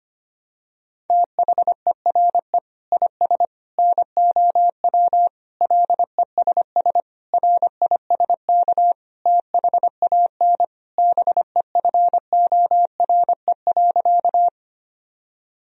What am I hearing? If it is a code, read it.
Morse code, heard as a keyed tone, about 25 wpm, 700 Hz. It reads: THERE IS NOW LESS RISK THAN BEFORE.
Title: Sentences from Top 1000 Words - Review Part 4 - 25wpm